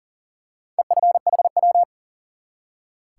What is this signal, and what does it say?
Morse code, heard as a keyed tone, about 42 wpm, 700 Hz. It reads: EFHW
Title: Top 180 QSO Elements 40wpm